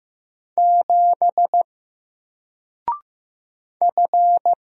Morse code, heard as a keyed tone, about 15 wpm, 700 Hz. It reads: 7 F
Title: Single Letter-Number Alt 15wpm